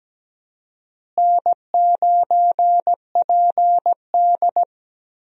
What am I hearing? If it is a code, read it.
Morse code, heard as a keyed tone, about 17 wpm, 700 Hz. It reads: N9PD
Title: USA Call Signs - 4 Characters - Repeated 3x - 17wpm